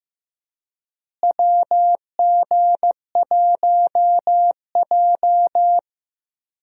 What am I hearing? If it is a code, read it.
Morse code, heard as a keyed tone, about 15 wpm, 700 Hz. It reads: WG1J
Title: USA Call Signs - 4 Characters - Repeated 3x - 15wpm